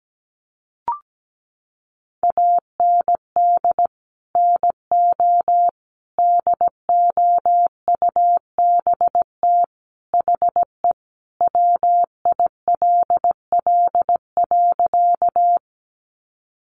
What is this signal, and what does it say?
Morse code, heard as a keyed tone, about 17 wpm, 700 Hz. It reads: AND NO DOUBT HE WILL.
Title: Sentences from Top 1500 Words 17wpm